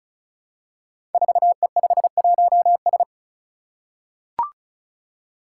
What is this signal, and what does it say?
Morse code, heard as a keyed tone, about 36 wpm, 700 Hz. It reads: 4E51S
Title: Five Challenging Letter-Number Combinations 35wpm